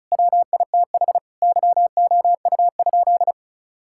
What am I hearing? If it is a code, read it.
Morse code, heard as a keyed tone, about 36 wpm, 700 Hz. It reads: WITH YOU?
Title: Sentences from Top 1000 Words - Review Part 6 - 35wpm